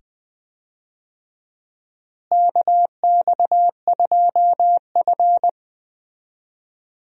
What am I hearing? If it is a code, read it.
Morse code, heard as a keyed tone, about 20 wpm, 700 Hz. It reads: KX2F
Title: USA Call Signs - 4 Characters - Repeated 2x - 20wpm